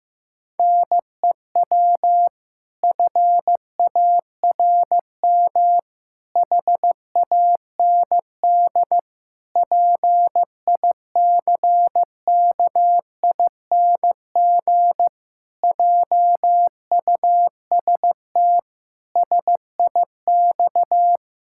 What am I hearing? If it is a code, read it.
Morse code, heard as a keyed tone, about 15 wpm, 700 Hz. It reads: NEW FARM HAND PICKING JUST SIX
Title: Pangram Challenge v2 - 15wpm